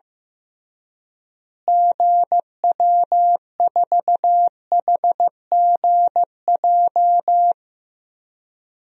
Morse code, heard as a keyed tone, about 15 wpm, 700 Hz. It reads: GW4HGJ
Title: DXCC-100 Call Signs - 6 Characters - Repeated 2x - 15wpm